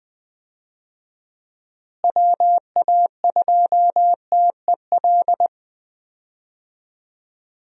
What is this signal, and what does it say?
Morse code, heard as a keyed tone, about 20 wpm, 700 Hz. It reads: WA2TEL
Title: USA Call Signs - 6 Characters - Repeated 2x - 20wpm